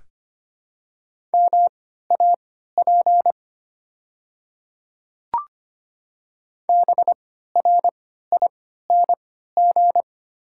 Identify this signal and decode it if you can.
Morse code, heard as a keyed tone, about 25 wpm, spaced as for 14 wpm, 700 Hz. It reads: MAP BRING
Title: 3x Character Spacing - Top 500 Words - 25wpm